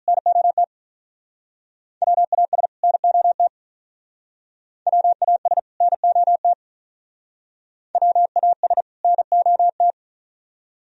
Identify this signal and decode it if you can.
Morse code, heard as a keyed tone, about 43 wpm, 700 Hz. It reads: NOT WAS NOT WAS NOT WAS NOT
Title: Speed-Racing - Sets of 2 Words 35wpm